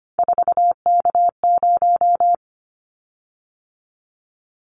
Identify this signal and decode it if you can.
Morse code, heard as a keyed tone, about 25 wpm, 700 Hz. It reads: 4K0
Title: Callsign Prefixes for Top 100 DXCC - 25wpm